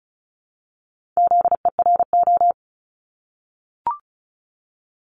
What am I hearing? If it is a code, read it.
Morse code, heard as a keyed tone, about 35 wpm, 700 Hz. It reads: ZERO
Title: Multi-letter Phonemes Repeated 3x 35wpm